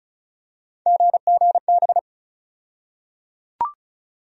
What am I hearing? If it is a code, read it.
Morse code, heard as a keyed tone, about 34 wpm, 700 Hz. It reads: GGB
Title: USA Call Sign Postfixes - 3 Letters 35wpm